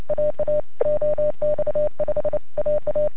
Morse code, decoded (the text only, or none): AA OX5AA